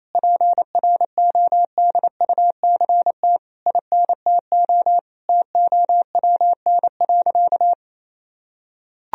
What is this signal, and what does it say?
Morse code, heard as a keyed tone, about 28 wpm, 700 Hz. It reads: PRODUCT INTO TOWN.
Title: Sentences from Top 1000 Words - Review Part 2 - 28wpm